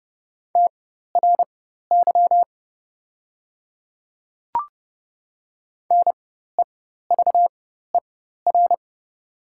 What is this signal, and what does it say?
Morse code, heard as a keyed tone, about 29 wpm, spaced as for 14 wpm, 700 Hz. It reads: TRY NEVER
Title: 4x Character Spacing - Top 500 Words - 30wpm